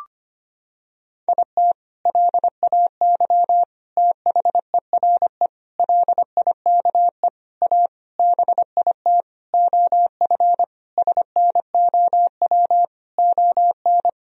Morse code, heard as a keyed tone, about 25 wpm, 700 Hz. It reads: IT LAY THERE LIKE A BIT OF SNOW ON
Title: Sentences from Top 1000 Words - Review Part 3 - 25wpm